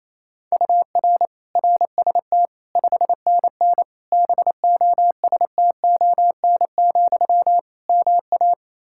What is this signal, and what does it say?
Morse code, heard as a keyed tone, about 28 wpm, 700 Hz. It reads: UR RST 5NN BOSTON, MA
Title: Practice QSO - Exchange 1 - Beginning - 28wpm